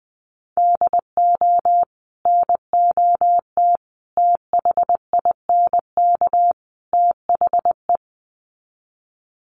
Morse code, heard as a keyed tone, about 20 wpm, 700 Hz. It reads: DO NOT THINK THE
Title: Sets of 4 Words 20wpm